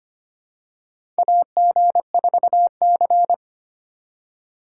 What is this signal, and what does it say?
Morse code, heard as a keyed tone, about 25 wpm, 700 Hz. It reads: AG4C
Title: USA Call Signs - 4 Characters - 2x Character Spacing - 25wpm